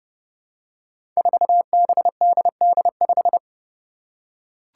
Morse code, heard as a keyed tone, about 30 wpm, 700 Hz. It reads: 4BDD5
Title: Five Challenging Letter-Number Combinations 30wpm